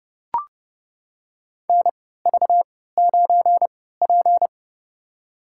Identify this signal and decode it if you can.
Morse code, heard as a keyed tone, about 30 wpm, spaced as for 17 wpm, 700 Hz. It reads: NV9P
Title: USA Call Signs - 3x Character Spacing - 30wpm